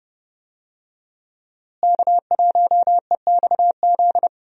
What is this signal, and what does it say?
Morse code, heard as a keyed tone, about 32 wpm, 700 Hz. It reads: K1EXZ
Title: USA Call Signs - 5 Characters - Repeated 3x - 30wpm